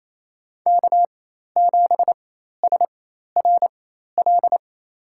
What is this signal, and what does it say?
Morse code, heard as a keyed tone, about 27 wpm, spaced as for 13 wpm, 700 Hz. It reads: K7SRL
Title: USA Call Signs - 5 Characters - 4x Character Spacing - 28wpm